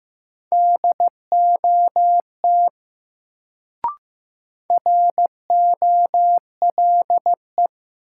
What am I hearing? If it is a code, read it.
Morse code, heard as a keyed tone, about 15 wpm, 700 Hz. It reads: DOT ROLE
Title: Sentences from Top 1500 Words - New Word Drill Alt - 15wpm